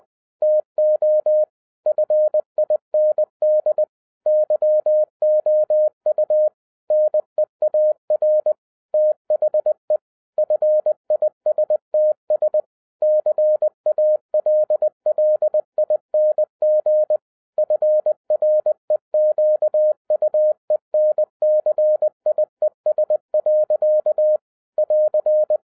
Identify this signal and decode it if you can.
Morse code, heard as a keyed tone, about 20 wpm, 610 Hz. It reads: TO FIND YOU NEAR THE FISTS CALLING FREQUENCIES. <AR>
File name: 2012-01-mp3 / n15-INVESTIGATION INTO THE POSSIBILITY OF WINNING PREFIX VERSION 3 AWARD-20wpm